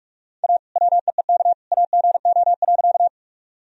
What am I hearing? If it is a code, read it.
Morse code, heard as a keyed tone, about 41 wpm, 700 Hz. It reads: A WEEK AGO.
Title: Sentences from Top 1000 Words - Review Part 3 - 45wpm